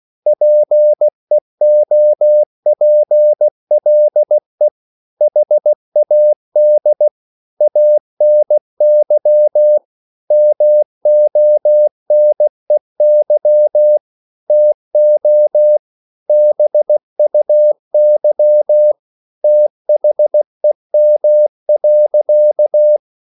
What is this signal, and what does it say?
Morse code, heard as a keyed tone, about 16 wpm, 600 Hz. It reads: PEOPLE HAD ANY MONEY TO BUY THEM.